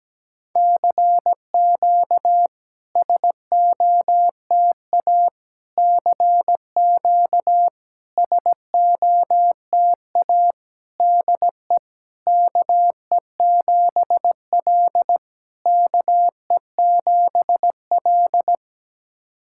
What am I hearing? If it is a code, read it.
Morse code, heard as a keyed tone, about 17 wpm, 700 Hz. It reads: CQ SOTA CQ SOTA DE KE7L KE7L
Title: SOTA QSO Practice 17wpm